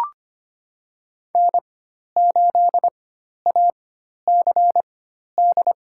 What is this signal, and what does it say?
Morse code, heard as a keyed tone, about 25 wpm, spaced as for 12 wpm, 700 Hz. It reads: N8ACD
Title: USA Call Signs - 4x Character Spacing - 25wpm